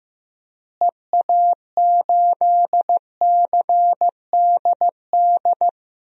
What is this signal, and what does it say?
Morse code, heard as a keyed tone, about 15 wpm, 700 Hz. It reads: EA8CDD
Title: DXCC-100 Call Signs - 6 Characters - Repeated 3x - 15wpm